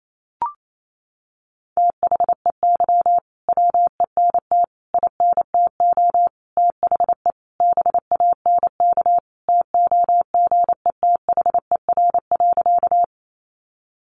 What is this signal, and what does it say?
Morse code, heard as a keyed tone, about 28 wpm, 700 Hz. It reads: THEY WENT INTO THE BANK TOGETHER.
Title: Sentences from Top 900 Words 28wpm